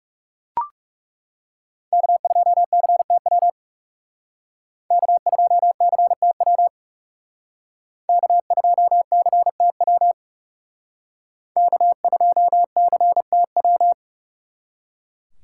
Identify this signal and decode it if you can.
Morse code, heard as a keyed tone, about 39 wpm, 700 Hz. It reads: K2CTW K2CTW K2CTW K2CTW